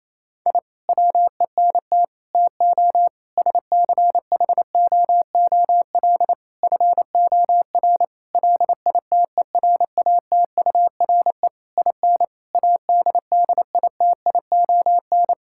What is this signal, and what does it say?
Morse code, heard as a keyed tone, about 27 wpm, 700 Hz. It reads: I WENT TO SCHOOL FOR LITERATURE IN ADDITION
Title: Sentences from Top 1000 Words - Review Part 6 - 28wpm